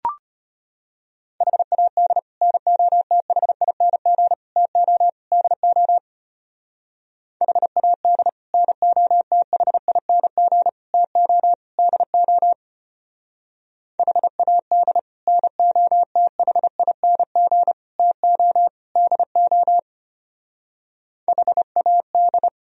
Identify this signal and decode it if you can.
Morse code, heard as a keyed tone, about 30 wpm, 700 Hz. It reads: HAD NOTHING TO DO HAD NOTHING TO DO HAD NOTHING TO DO HAD
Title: Speed-Racing - Sets of 4 Words 25wpm